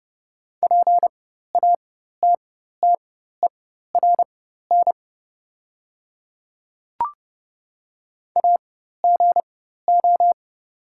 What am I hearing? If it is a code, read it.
Morse code, heard as a keyed tone, about 30 wpm, spaced as for 14 wpm, 690 Hz. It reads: PATTERN AGO
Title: 4x Character Spacing - Top 500 Words - 30wpm